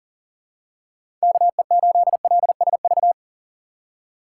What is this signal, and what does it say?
Morse code, heard as a keyed tone, about 39 wpm, 700 Hz. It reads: KE8LSV
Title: USA Call Signs - 3x Character Spacing - 40wpm